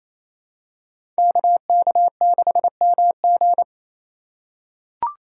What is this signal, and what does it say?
Morse code, heard as a keyed tone, about 28 wpm, 700 Hz. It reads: KK6MG
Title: USA Call Signs - 5 Characters - 4x Character Spacing - 28wpm